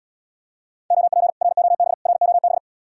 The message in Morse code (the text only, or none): OP ROB ROB